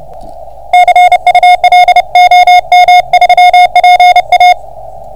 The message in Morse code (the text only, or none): CUL OM3PA